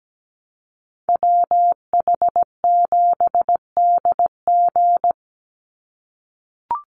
Morse code, heard as a keyed tone, about 17 wpm, 700 Hz. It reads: WH7DG
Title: USA Call Signs 17wpm